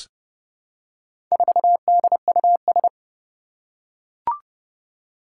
Morse code, heard as a keyed tone, about 30 wpm, 700 Hz. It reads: 4DUS
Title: Four Challenging Letter-Number Combinations 30wpm